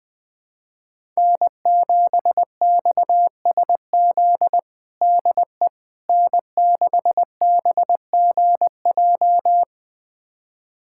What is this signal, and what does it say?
Morse code, heard as a keyed tone, about 20 wpm, 700 Hz. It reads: N7XSZ DE N6BGJ
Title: USA Callsign DE Callsign - 5 Characters - 20wpm